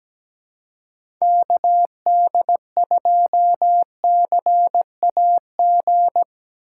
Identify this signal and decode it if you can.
Morse code, heard as a keyed tone, about 17 wpm, 700 Hz. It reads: KD2CAG
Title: USA Call Signs - 6 Characters - 4x Character Spacing - 17wpm